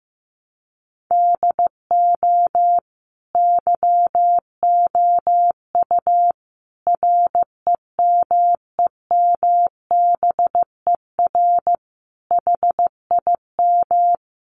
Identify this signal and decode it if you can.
Morse code, heard as a keyed tone, about 15 wpm, 700 Hz. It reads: DO YOU REMEMBER HIM
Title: Sentences from Top 1300 Words 15wpm